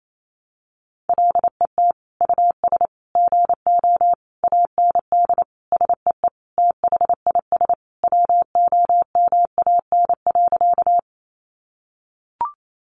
Morse code, heard as a keyed tone, about 28 wpm, 700 Hz. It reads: LET US GO AND SEE THIS WOMAN.
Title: Sentences from Top 1100 Words 28wpm